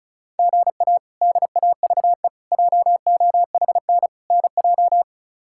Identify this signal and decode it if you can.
Morse code, heard as a keyed tone, about 36 wpm, 700 Hz. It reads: GA DAVE JOHN NJ
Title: K1USN SST QSO Practice - 35wpm